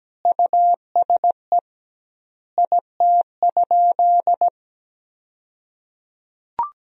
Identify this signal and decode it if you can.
Morse code, heard as a keyed tone, about 17 wpm, 700 Hz. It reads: USE IT?
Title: 2x Word-Spacing - Sentences from Top 200 Words - 17wpm